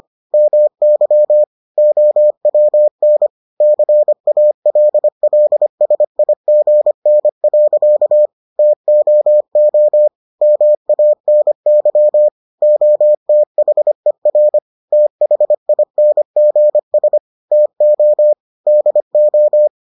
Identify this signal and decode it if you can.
Morse code, heard as a keyed tone, about 25 wpm, 600 Hz. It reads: MY OWN CALLSIGN. TOO MANY OTHER THINGS TO DO